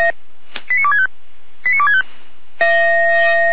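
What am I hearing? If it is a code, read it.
Five-tone selective call (ZVEI), heard as quick runs of five tones, tones of about 70 ms. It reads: 89265 89265
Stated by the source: ZVEI five-tone call received on SDR, Feuerwehr Orlamünde.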